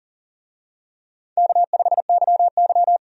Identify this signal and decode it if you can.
Morse code, heard as a keyed tone, about 40 wpm, 700 Hz. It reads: K5YY
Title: USA Call Signs - 4 Characters - 40wpm